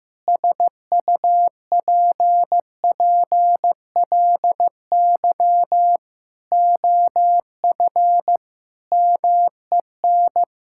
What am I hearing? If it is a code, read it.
Morse code, heard as a keyed tone, about 15 wpm, 700 Hz. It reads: SUPPLY OF MEN
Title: Sentences from Top 1000 Words - Review Part 6 - 15wpm